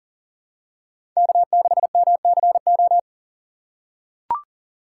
Morse code, heard as a keyed tone, about 40 wpm, 700 Hz. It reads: K6MCO
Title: USA Call Signs - 5 Characters - 2x Character Spacing - 40wpm